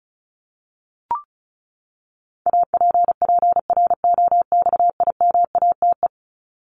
Morse code, heard as a keyed tone, about 34 wpm, 700 Hz. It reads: APPROXIMATE
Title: LY Suffix - Top 100 Words with Base Words 35wpm